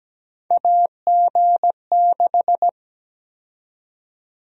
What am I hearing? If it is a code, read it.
Morse code, heard as a keyed tone, about 17 wpm, 700 Hz. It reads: AG6
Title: USA Call Sign Prefixes - 17wpm